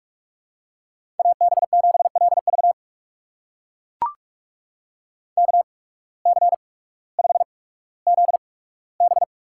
Morse code, heard as a keyed tone, about 44 wpm, spaced as for 19 wpm, 700 Hz. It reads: AB7LV KC5ZB
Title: USA Call Signs - 5 Characters - 8x Character Spacing - 45wpm